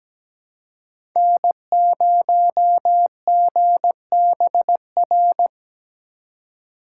Morse code, heard as a keyed tone, about 17 wpm, 700 Hz. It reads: N0GBR
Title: USA Call Signs - 2x Character Spacing - 17wpm